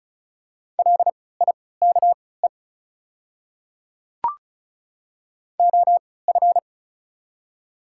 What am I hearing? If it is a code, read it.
Morse code, heard as a keyed tone, about 36 wpm, spaced as for 20 wpm, 700 Hz. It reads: LIKE OF
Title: 3x Character Spacing - Top 200 Words - 35wpm